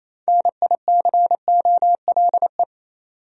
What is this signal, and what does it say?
Morse code, heard as a keyed tone, about 28 wpm, 700 Hz. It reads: NICOLE
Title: USA Common Names - 28wpm